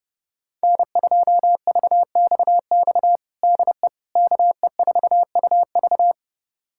Morse code, heard as a keyed tone, about 30 wpm, 700 Hz. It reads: N2VXX DE KE4UV
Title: USA Callsign DE Callsign - 5 Characters - 30wpm